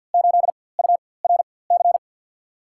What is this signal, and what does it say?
Morse code, heard as a keyed tone, about 53 wpm, spaced as for 24 wpm, 700 Hz. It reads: 8 URC
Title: USA Call Signs - 6 Characters - 4x Character Spacing - 50wpm